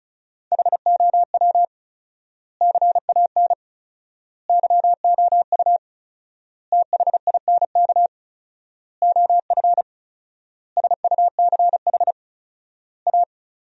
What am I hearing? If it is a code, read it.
Morse code, heard as a keyed tone, about 34 wpm, 700 Hz. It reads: HOW CAN YOU THINK OF SUCH A
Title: 4x Word-Spacing - Sentences from Top 200 Words - 35wpm